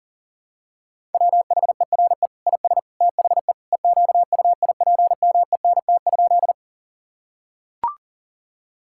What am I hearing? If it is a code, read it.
Morse code, heard as a keyed tone, about 40 wpm, 700 Hz. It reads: WHERE IS THE EQUIPMENT?
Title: Sentences from Top 1000 Words - Review Part 5 - 40wpm